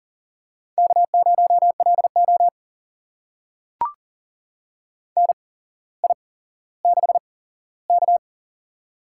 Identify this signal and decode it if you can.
Morse code, heard as a keyed tone, about 39 wpm, spaced as for 17 wpm, 700 Hz. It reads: K0LO NI6K
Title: USA Call Signs - 4 Characters - 8x Character Spacing - 40wpm